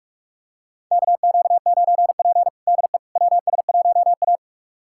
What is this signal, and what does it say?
Morse code, heard as a keyed tone, about 46 wpm, 700 Hz. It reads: KQ9P DE WS1A
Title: USA Callsign DE Callsign - 4 Characters - 45wpm